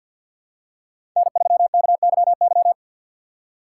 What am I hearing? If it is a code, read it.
Morse code, heard as a keyed tone, about 53 wpm, 700 Hz. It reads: N3KYY
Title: USA Call Signs - 5 Characters - 8x Character Spacing - 50wpm